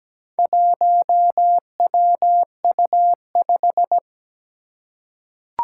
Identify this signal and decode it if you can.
Morse code, heard as a keyed tone, about 17 wpm, 700 Hz. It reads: 1WU5
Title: Four Challenging Letter-Number Combinations 17wpm